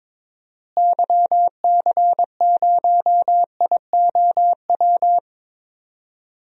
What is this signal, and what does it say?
Morse code, heard as a keyed tone, about 22 wpm, 700 Hz. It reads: YC0IOW
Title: DXCC-100 Call Signs - 6 Characters - Repeated 3x - 22wpm